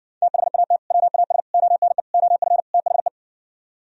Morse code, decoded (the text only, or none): THAT WAS ONE OF THE